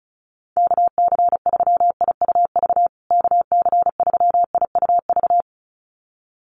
Morse code, heard as a keyed tone, about 34 wpm, 700 Hz. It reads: KC3IUV KC3IUV
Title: USA Call Signs Repeated 2x 35wpm